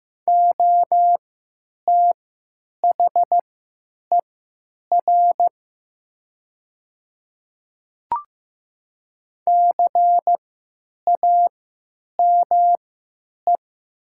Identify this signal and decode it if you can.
Morse code, heard as a keyed tone, about 15 wpm, spaced as for 8 wpm, 700 Hz. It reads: OTHER CAME
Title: 3x Character Spacing - Top 200 Words - 15wpm